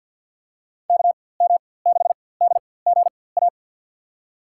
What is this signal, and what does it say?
Morse code, heard as a keyed tone, about 53 wpm, spaced as for 24 wpm, 700 Hz. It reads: KM6DGA
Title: USA Call Signs - 6 Characters - 4x Character Spacing - 50wpm